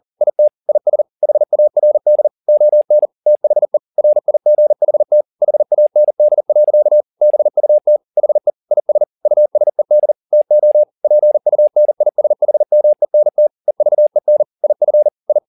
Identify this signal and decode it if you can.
Morse code, heard as a keyed tone, about 40 wpm, 600 Hz. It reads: IT IS HARD ON THE RIGHT HAND. BUT HE IS USED TO PUNISHMENT EVEN IF I
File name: the old man and the sea 2 40wpm